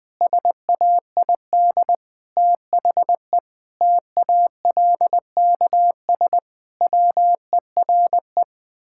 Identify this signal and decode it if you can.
Morse code, heard as a keyed tone, about 20 wpm, 700 Hz. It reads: SAID THE TALKS WERE
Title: Sentences with 8-Word Phrases - 20wpm